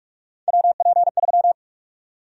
Morse code, heard as a keyed tone, about 46 wpm, 700 Hz. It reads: WP3
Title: USA Call Sign Prefixes - 45wpm